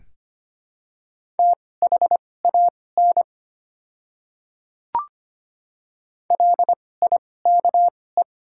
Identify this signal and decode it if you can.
Morse code, heard as a keyed tone, about 25 wpm, spaced as for 18 wpm, 700 Hz. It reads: THAN LIKE
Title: 2x Character Spacing - Top 100 Words - 25wpm